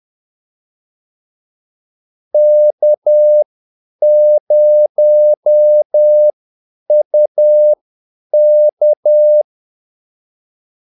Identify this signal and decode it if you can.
Morse code, heard as a keyed tone, about 10 wpm, spaced as for 8 wpm, 600 Hz. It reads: K0UK
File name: Callsigns-10-8wpm